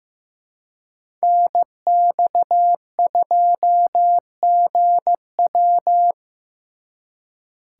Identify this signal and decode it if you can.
Morse code, heard as a keyed tone, about 18 wpm, 700 Hz. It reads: NX2GW
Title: Speed Racing - USA Call Signs 15wpm